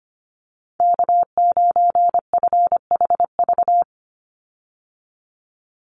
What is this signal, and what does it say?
Morse code, heard as a keyed tone, about 25 wpm, 700 Hz. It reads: K9FHV
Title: USA Call Signs 25wpm